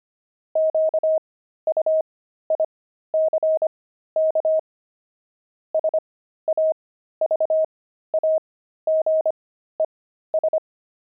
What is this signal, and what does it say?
Morse code, heard as a keyed tone, about 25 wpm, spaced as for 13 wpm, 630 Hz. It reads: QUICK SAVAGES
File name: pangram-fs-13-cs-25-javelins